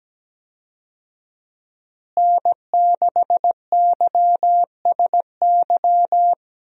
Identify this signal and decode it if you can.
Morse code, heard as a keyed tone, about 17 wpm, 700 Hz. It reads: N6YSY
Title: USA Call Signs - 5 Characters - Repeated 2x - 17wpm